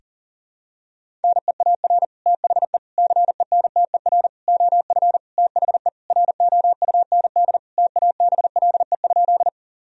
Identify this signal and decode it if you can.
Morse code, heard as a keyed tone, about 40 wpm, 700 Hz. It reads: NEAR THE CENTER OF THE ROUND TABLE?
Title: Sentences from Top 1000 Words - Review Part 2 - 40wpm